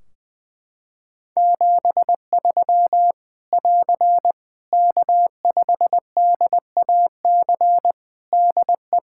Morse code, heard as a keyed tone, about 21 wpm, 700 Hz. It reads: 73 <AR> K5DAC DE
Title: Practice QSO - Exchange 4 - Ending - 20wpm